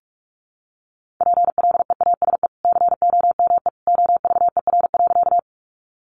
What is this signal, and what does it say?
Morse code, heard as a keyed tone, about 44 wpm, 700 Hz. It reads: PLEASE COME OVER.
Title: Sentences from Top 1500 Words 45wpm